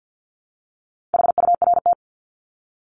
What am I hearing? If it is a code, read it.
Morse code, heard as a keyed tone, about 48 wpm, 700 Hz. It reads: HURT
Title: Top 1751-2000 Words 50wpm